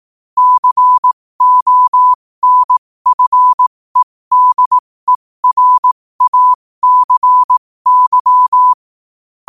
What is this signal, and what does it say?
Morse code, heard as a keyed tone, about 18 wpm, spaced as for 15 wpm, 1000 Hz. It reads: CONFEDERACY
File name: qotdfiles / 15wpm